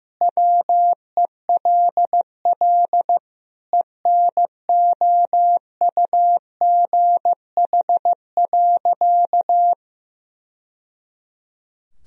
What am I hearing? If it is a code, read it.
Morse code, heard as a keyed tone, about 15 wpm, 700 Hz. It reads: WELL ENOUGH.
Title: Sentences from Top 1000 Words - Review Part 3 - 15wpm